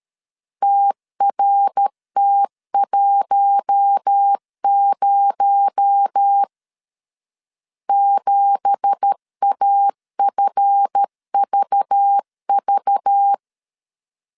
Morse code, heard as a keyed tone, about 13 wpm, 800 Hz. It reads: TRT10 7AFVV
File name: CW / DET / 1 9 1500C S A 12wpm 3800